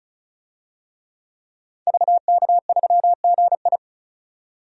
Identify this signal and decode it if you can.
Morse code, heard as a keyed tone, about 34 wpm, 700 Hz. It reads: VK3GI
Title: DXCC-100 Call Signs - 5 Characters - 35wpm